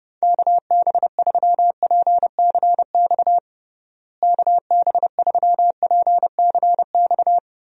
Morse code, heard as a keyed tone, about 30 wpm, 700 Hz. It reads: KB3PCX KB3PCX
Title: USA Call Signs - 6 Characters - Repeated 3x - 30wpm